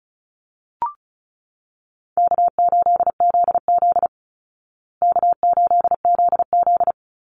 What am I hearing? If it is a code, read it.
Morse code, heard as a keyed tone, about 34 wpm, 700 Hz. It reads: K8ZZ K8ZZ
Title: Easy Call Signs Repeated 3x - 35wpm